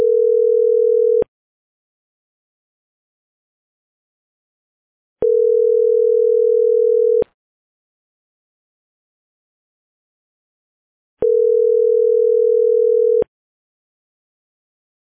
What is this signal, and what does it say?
Ringback tone (North America), 440 + 480 Hz.